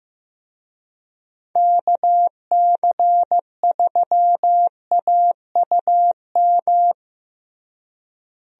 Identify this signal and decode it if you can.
Morse code, heard as a keyed tone, about 15 wpm, 700 Hz. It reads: KC3AUM
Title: USA Call Signs - 6 Characters - Repeated 2x - 15wpm